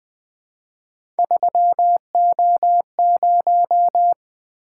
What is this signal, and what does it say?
Morse code, heard as a keyed tone, about 20 wpm, 700 Hz. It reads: 3O0
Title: Three Letter-Number Combinations - 20wpm